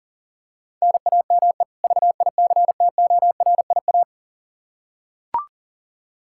Morse code, heard as a keyed tone, about 40 wpm, 700 Hz. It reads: NAME VICTORIA
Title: Practice QSO - Exchange 1 - Beginning - 40wpm